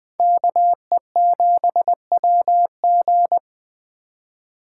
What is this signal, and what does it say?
Morse code, heard as a keyed tone, about 21 wpm, 700 Hz. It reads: KE7WG